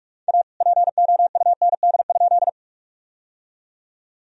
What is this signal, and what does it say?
Morse code, heard as a keyed tone, about 46 wpm, 700 Hz. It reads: A POUND?
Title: Sentences from Top 1000 Words - Review Part 2 - 45wpm